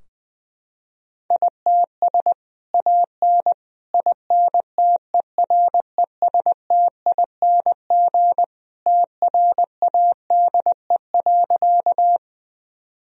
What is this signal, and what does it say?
Morse code, heard as a keyed tone, about 21 wpm, 700 Hz. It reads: ITS AN INTERESTING TRADE.
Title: Sentences from Top 1000 Words - Review Part 4 - 20wpm